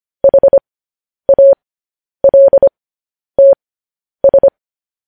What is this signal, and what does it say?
Morse code, heard as a keyed tone, about 25 wpm, spaced as for 10 wpm, 550 Hz. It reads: HALTS